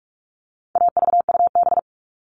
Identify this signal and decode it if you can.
Morse code, heard as a keyed tone, about 44 wpm, 700 Hz. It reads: AVUB
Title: All Cut Numbers - 4-Digit 45wpm